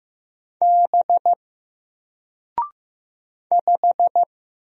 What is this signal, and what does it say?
Morse code, heard as a keyed tone, about 15 wpm, 700 Hz. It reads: B 5
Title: Single Letter-Number 15wpm